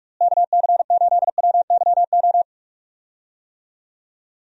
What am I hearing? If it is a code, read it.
Morse code, heard as a keyed tone, about 44 wpm, 700 Hz. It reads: KC8WYO
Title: USA Call Signs - 6 Characters - Repeated 2x - 45wpm